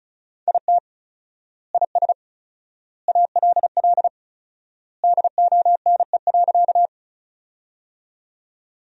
Morse code, heard as a keyed tone, about 34 wpm, 700 Hz. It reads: IT IS ALL DONE.
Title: 4x Word-Spacing - Sentences from Top 500 Words - 35wpm